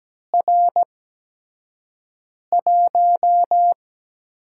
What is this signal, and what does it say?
Morse code, heard as a keyed tone, about 17 wpm, spaced as for 5 wpm, 700 Hz. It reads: R1
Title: USA Call Signs - 4 Characters - 8x Character Spacing - 17wpm